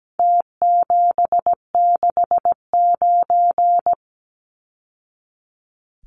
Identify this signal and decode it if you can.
Morse code, heard as a keyed tone, about 17 wpm, 700 Hz. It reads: T769
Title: Numbers - Contest Style - 4-Digit 17wpm